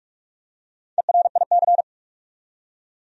Morse code, heard as a keyed tone, about 46 wpm, 700 Hz. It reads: ERIC
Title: USA Common Names - Max 5 Letters - 45wpm